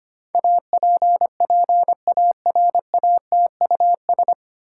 Morse code, heard as a keyed tone, about 25 wpm, 700 Hz. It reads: APPARATUS
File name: practice / 25wpm / 9 Letter Words - First 3 Letters Given - 25wpm